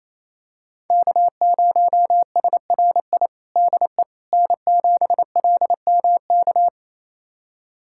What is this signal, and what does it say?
Morse code, heard as a keyed tone, about 28 wpm, 700 Hz. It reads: K0SRI DE N7LMK
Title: USA Callsign DE Callsign - 5 Characters - 28wpm